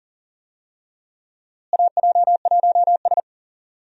Morse code, heard as a keyed tone, about 40 wpm, 700 Hz. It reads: AJ1S
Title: USA Call Signs - 4 Characters - Repeated 2x - 40wpm